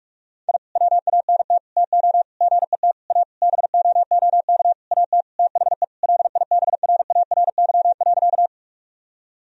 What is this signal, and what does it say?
Morse code, heard as a keyed tone, about 43 wpm, 700 Hz. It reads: I WANT TO GET A BOOK AT THE LIBRARY.
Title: Sentences from Top 1000 Words - Review Part 1 - 45wpm